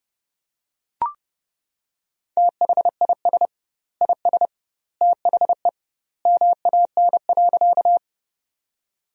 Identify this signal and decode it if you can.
Morse code, heard as a keyed tone, about 30 wpm, 700 Hz. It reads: THIS IS THE MAN.
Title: 2x Word-Spacing - Sentences from Top 200 Words - 30wpm